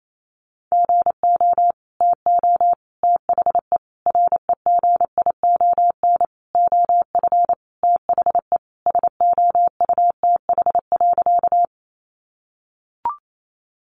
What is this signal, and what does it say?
Morse code, heard as a keyed tone, about 28 wpm, 700 Hz. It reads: GO TO THE REGION OF THE SOUTH.
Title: Sentences from Top 1500 Words 28wpm